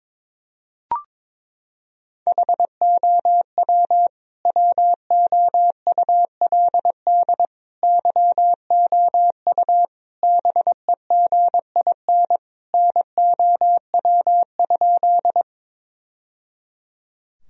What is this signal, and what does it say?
Morse code, heard as a keyed tone, about 22 wpm, 700 Hz. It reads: HOW WOULD YOU BEGIN NOW?
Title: Sentences from Top 1000 Words - Review Part 1 - 22wpm